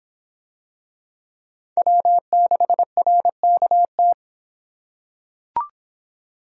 Morse code, heard as a keyed tone, about 26 wpm, 700 Hz. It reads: W6RKT